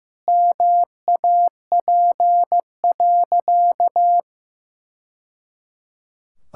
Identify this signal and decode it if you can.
Morse code, heard as a keyed tone, about 15 wpm, 700 Hz. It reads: MAP.
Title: Sentences from Top 1000 Words - Review Part 6 - 15wpm